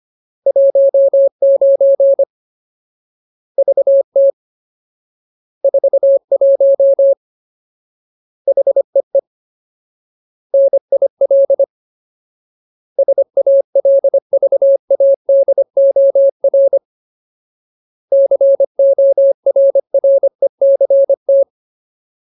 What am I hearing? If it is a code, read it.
Morse code, heard as a keyed tone, about 25 wpm, 550 Hz. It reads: 19 VT 41 HEE NIL SALVADOR CORRECT
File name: Ultimate Word List / 25WPM Ultimate